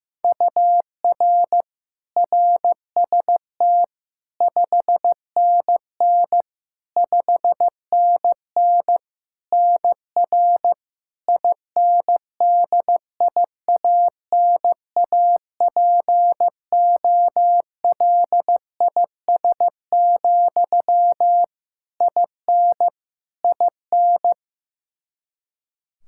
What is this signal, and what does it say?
Morse code, heard as a keyed tone, about 15 wpm, 700 Hz. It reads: UR RST 5NN 5NN NR INDIANAPOLIS, IN IN
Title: K1USN SST QSO Practice 15wpm